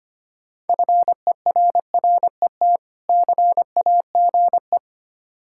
Morse code, heard as a keyed tone, about 25 wpm, 700 Hz. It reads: FERRET CAGE
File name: practice / 25wpm / Difficult Words with Hints - 25wpm